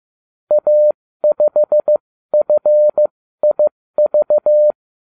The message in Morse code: A5FIV

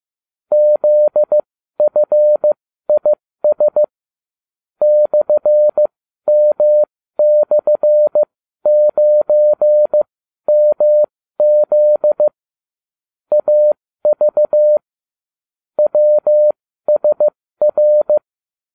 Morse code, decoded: ZFIS /M/9MZ AV WSR

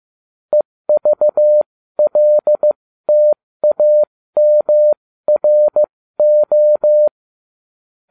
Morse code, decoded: EVLTAMRO

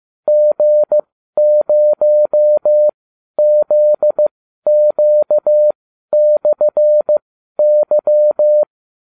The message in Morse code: G0ZQ/Y

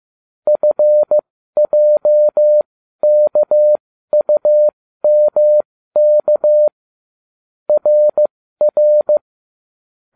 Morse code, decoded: FJKUMK RR

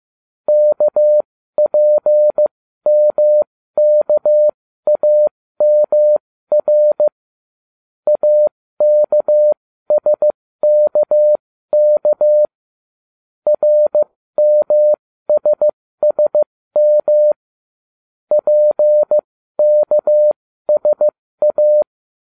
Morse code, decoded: KPMKAMR AKSKK RMSSM PKSA

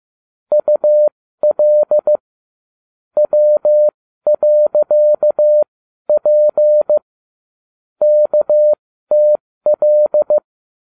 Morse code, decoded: UL W.P KTL

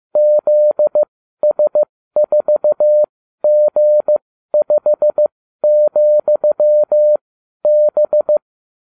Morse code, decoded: ZS4G5,B

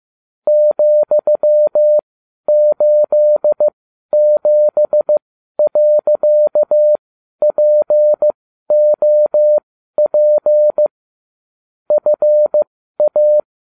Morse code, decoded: ,87.POP FA

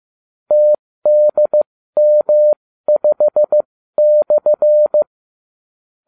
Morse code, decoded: TDM5/

